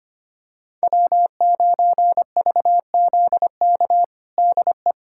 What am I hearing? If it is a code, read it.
Morse code, heard as a keyed tone, about 25 wpm, 700 Hz. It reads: W9VZK DE
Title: USA Callsign DE Callsign - 5 Characters - 25wpm